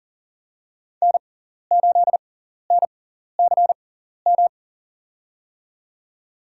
Morse code, heard as a keyed tone, about 40 wpm, spaced as for 14 wpm, 700 Hz. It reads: N8NCM
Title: USA Call Signs - 5 Characters - 6x Character Spacing - 40wpm